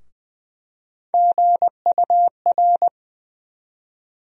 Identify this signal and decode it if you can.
Morse code, heard as a keyed tone, about 20 wpm, 700 Hz. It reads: GUR
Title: USA Call Sign Postfixes - 3 Letters 20wpm